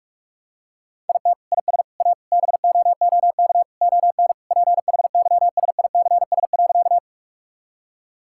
Morse code, heard as a keyed tone, about 42 wpm, 700 Hz. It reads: IT IS A BOOK ON PHYSICS.
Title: Sentences from Top 1000 Words - Review Part 5 - 45wpm